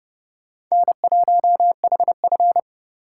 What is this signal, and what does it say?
Morse code, heard as a keyed tone, about 29 wpm, 700 Hz. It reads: N1HF